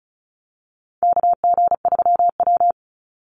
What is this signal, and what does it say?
Morse code, heard as a keyed tone, about 34 wpm, 700 Hz. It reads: KG3W
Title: NA SOTA Chaser Callsigns 35wpm